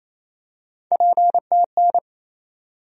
Morse code, heard as a keyed tone, about 28 wpm, 700 Hz. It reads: PTN
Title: USA Call Sign Postfixes - 3 Letters 28wpm